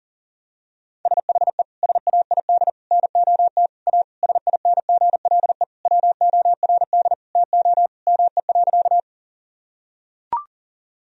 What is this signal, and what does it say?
Morse code, heard as a keyed tone, about 40 wpm, 700 Hz. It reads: SHE SAID NOT A SINGLE WORD TO ME.
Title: Sentences from Top 1000 Words - Review Part 4 - 40wpm